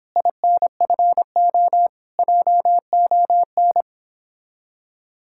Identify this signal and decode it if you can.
Morse code, heard as a keyed tone, about 29 wpm, 700 Hz. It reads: INFO JON